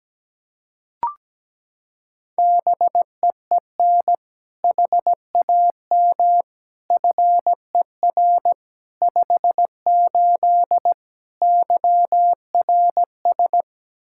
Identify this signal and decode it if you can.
Morse code, heard as a keyed tone, about 17 wpm, 700 Hz. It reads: BEEN HAM FER 58 YRS
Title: Practice QSO - Exchange 3 - More Info - 17wpm